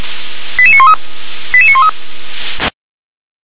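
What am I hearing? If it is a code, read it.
Five-tone selective call (ZVEI), heard as quick runs of five tones, tones of about 70 ms. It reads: 88913 88913
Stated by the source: ZVEI five-tone call received by SDR.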